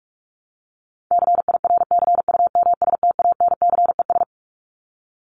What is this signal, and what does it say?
Morse code, heard as a keyed tone, about 44 wpm, 700 Hz. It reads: CIRCUMSTANCES
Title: ES:S Suffix - Top 100 Words with Base Words 45wpm